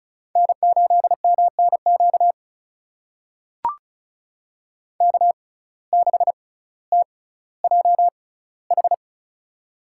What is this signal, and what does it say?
Morse code, heard as a keyed tone, about 36 wpm, spaced as for 18 wpm, 700 Hz. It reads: N8MNQ K6TJH
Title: USA Call Signs - 6x Character Spacing - 35wpm